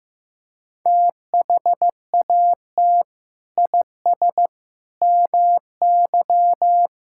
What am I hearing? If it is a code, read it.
Morse code, heard as a keyed tone, about 15 wpm, 700 Hz. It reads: THAT IS MY
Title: Sentences from Top 1000 Words - Review Part 5 - 15wpm